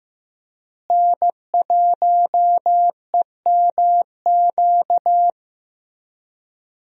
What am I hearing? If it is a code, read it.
Morse code, heard as a keyed tone, about 15 wpm, 700 Hz. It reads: N1EMQ
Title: USA Call Signs - 5 Characters - Repeated 2x - 15wpm